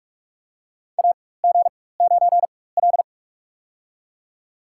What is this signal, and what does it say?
Morse code, heard as a keyed tone, about 46 wpm, spaced as for 21 wpm, 700 Hz. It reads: AG9L
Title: USA Call Signs - 4 Characters - 4x Character Spacing - 45wpm